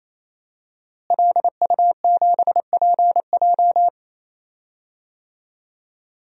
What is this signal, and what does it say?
Morse code, heard as a keyed tone, about 28 wpm, 700 Hz. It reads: LU7PJ
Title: DXCC-100 Call Signs - 5 Characters - Repeated 3x - 28wpm